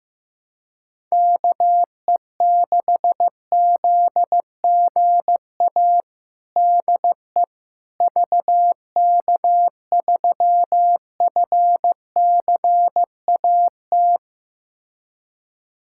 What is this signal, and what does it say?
Morse code, heard as a keyed tone, about 15 wpm, 700 Hz. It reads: KE6ZGA DE VK3FCAT
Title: USA Callsign DE Callsign 15wpm